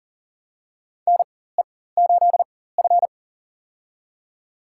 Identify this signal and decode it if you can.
Morse code, heard as a keyed tone, about 40 wpm, spaced as for 19 wpm, 700 Hz. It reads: NE8F
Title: USA Call Signs - 4 Characters - 4x Character Spacing - 40wpm